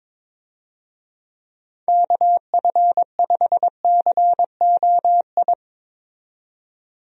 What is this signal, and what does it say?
Morse code, heard as a keyed tone, about 22 wpm, 700 Hz. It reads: KF5COI
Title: USA Call Signs - 6 Characters - 22wpm